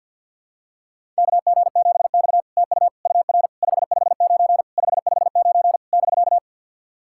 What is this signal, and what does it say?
Morse code, heard as a keyed tone, about 51 wpm, 700 Hz. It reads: KG7X TU UR 559 559 <BK>
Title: SOTA QSO Practice 50wpm